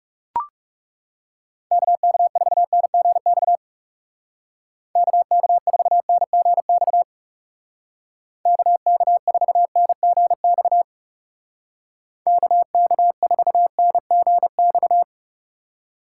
Morse code, heard as a keyed tone, about 40 wpm, 700 Hz. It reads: KK4NGX KK4NGX KK4NGX KK4NGX